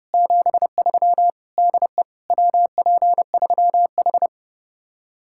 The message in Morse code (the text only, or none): 73 DE WP3H